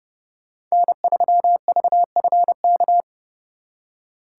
Morse code, heard as a keyed tone, about 37 wpm, 700 Hz. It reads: N3VFK
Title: Speed Racing - USA 1x3 Call Signs 30wpm